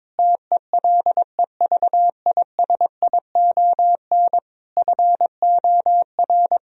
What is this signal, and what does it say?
Morse code, heard as a keyed tone, about 22 wpm, 700 Hz. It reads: TELEVISION FOR
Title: Sentences from Top 1000 Words - Review Part 4 - 22wpm